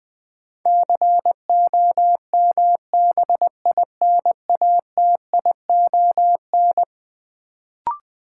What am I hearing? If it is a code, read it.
Morse code, heard as a keyed tone, about 21 wpm, 700 Hz. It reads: COMBINATION
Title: Sentences from Top 1300 Words - New Word Drill Alt - 20wpm